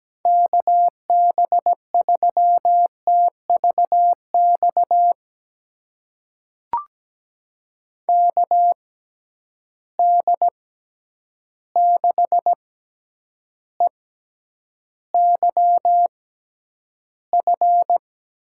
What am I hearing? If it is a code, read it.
Morse code, heard as a keyed tone, about 17 wpm, spaced as for 9 wpm, 700 Hz. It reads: KB3TVX KD6EYF
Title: USA Call Signs - 6x Character Spacing - 17wpm